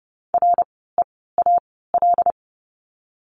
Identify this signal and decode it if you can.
Morse code, heard as a keyed tone, about 30 wpm, spaced as for 17 wpm, 700 Hz. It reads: REAL
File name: course / 30:10wpm / 108 Lesson 18f - Letter m - Words 30wpm